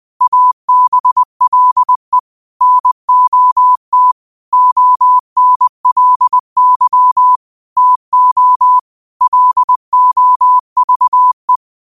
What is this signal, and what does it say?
Morse code, heard as a keyed tone, about 20 wpm, 1000 Hz. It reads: ABLE NOT ONLY TO LOVE
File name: qotdfiles / 20wpm